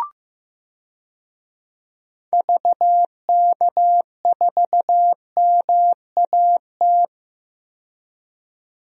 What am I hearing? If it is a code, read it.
Morse code, heard as a keyed tone, about 15 wpm, 700 Hz. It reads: VK4MAT
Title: DXCC-100 Call Signs - 6 Characters - 15wpm